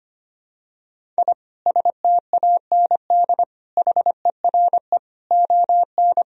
Morse code, heard as a keyed tone, about 25 wpm, 700 Hz. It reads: I STAND HERE ON
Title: Sentences from Top 1000 Words - Review Part 1 - 25wpm